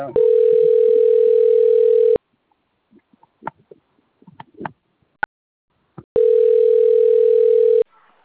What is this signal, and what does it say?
Ringback tone (North America), 440 + 480 Hz.
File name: rg-1600-unknown- - - .39741